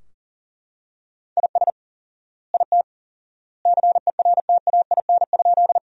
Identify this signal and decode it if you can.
Morse code, heard as a keyed tone, about 40 wpm, 700 Hz. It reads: IS IT CERTAIN?
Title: 4x Word-Spacing - Sentences from Top 500 Words - 40wpm